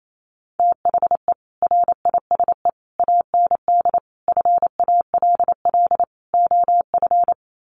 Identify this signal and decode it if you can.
Morse code, heard as a keyed tone, about 28 wpm, 700 Hz. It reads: THE RISE AND FALL OF
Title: Sets of 5 Words Encore2 28wpm